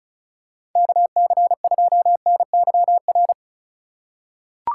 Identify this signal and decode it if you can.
Morse code, heard as a keyed tone, about 34 wpm, 700 Hz. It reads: KC2NYR
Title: USA Call Signs - 6 Characters - 3x Character Spacing - 35wpm